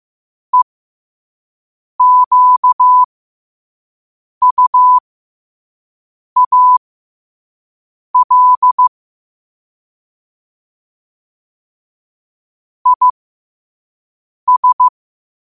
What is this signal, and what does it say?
Morse code, heard as a keyed tone, about 15 wpm, spaced as for 5 wpm, 1000 Hz. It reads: EQUAL IS